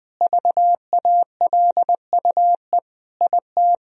Morse code, heard as a keyed tone, about 20 wpm, 700 Hz. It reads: VALUE IT
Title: Sentences from Top 1000 Words - Review Part 3 - 20wpm